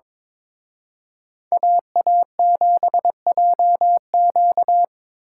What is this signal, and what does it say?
Morse code, heard as a keyed tone, about 26 wpm, 700 Hz. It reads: AA7JQ
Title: Speed Racing - USA 2x2 Call Signs 22wpm